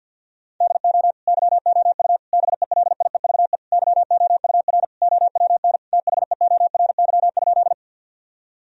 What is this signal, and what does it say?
Morse code, heard as a keyed tone, about 50 wpm, 700 Hz. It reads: DO YOU BELIEVE YOUR OWN THEORY?
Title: Sentences from Top 1000 Words - Review Part 3 - 50wpm